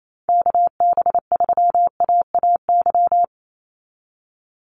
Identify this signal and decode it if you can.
Morse code, heard as a keyed tone, about 28 wpm, 700 Hz. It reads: KB3AAY
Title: Call Signs Repeated 2x - 28wpm